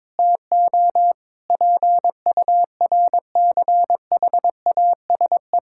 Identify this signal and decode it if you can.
Morse code, heard as a keyed tone, about 22 wpm, 700 Hz. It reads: TO PURCHASE
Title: Sentences from Top 1000 Words - Review Part 5 - 22wpm